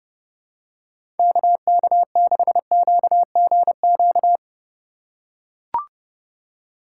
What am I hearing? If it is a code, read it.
Morse code, heard as a keyed tone, about 30 wpm, 700 Hz. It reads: KK6QGQ
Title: USA Call Signs - 6 Characters - 4x Character Spacing - 30wpm